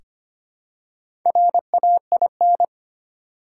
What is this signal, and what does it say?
Morse code, heard as a keyed tone, about 25 wpm, 700 Hz. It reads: RAIN
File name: practice / 25wpm / Easy Words with Hints - 25wpm